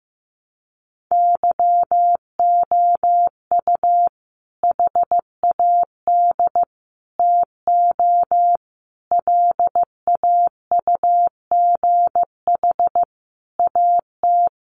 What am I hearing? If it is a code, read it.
Morse code, heard as a keyed tone, about 15 wpm, 700 Hz. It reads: YOU HAD TO LAUGH AT
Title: Sentences from Top 600 Words 15wpm